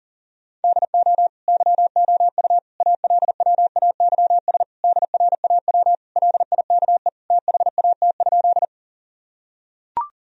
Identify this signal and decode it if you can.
Morse code, heard as a keyed tone, about 40 wpm, 700 Hz. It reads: DO YOU ALWAYS DRAW LIKE THAT?
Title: Sentences from Top 1000 Words - Review Part 1 - 40wpm